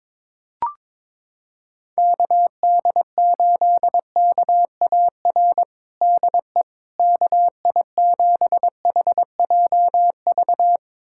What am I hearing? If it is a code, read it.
Morse code, heard as a keyed tone, about 21 wpm, 700 Hz. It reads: KD8KAR DE KI7HJV
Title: USA Callsign DE Callsign - 6 Characters - 22wpm